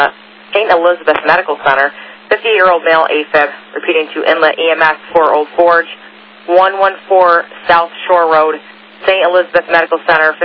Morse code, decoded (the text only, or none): none